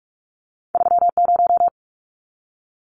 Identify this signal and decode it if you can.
Morse code, heard as a keyed tone, about 44 wpm, 700 Hz. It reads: 30
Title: Numbers - 2-Digit 45wpm